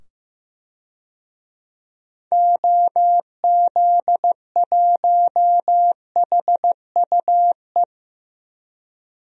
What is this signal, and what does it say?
Morse code, heard as a keyed tone, about 15 wpm, 700 Hz. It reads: OZ1HUE
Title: DXCC-100 Call Signs - 6 Characters - 15wpm